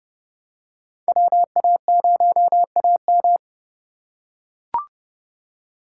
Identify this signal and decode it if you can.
Morse code, heard as a keyed tone, about 30 wpm, 700 Hz. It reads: WA0AM
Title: USA Call Signs - 4x Character Spacing - 30wpm